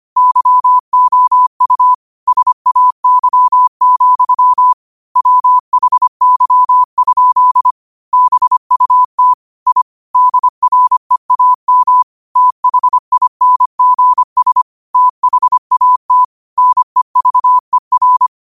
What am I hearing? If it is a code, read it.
Morse code, heard as a keyed tone, about 25 wpm, 1000 Hz. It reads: YOU SAY, WHY? BUT I DREAM THINGS THAT NEVER